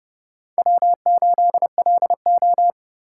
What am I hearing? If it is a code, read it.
Morse code, heard as a keyed tone, about 29 wpm, 700 Hz. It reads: W8LO
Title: Speed Racing - USA Call Signs 25wpm